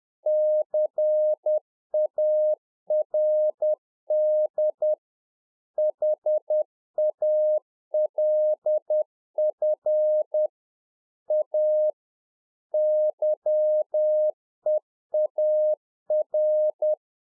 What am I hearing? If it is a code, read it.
Morse code, heard as a keyed tone, about 10 wpm, 610 Hz. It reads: CARD HALF A YEAR
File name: UNFOGETTABLE QSO 10WPM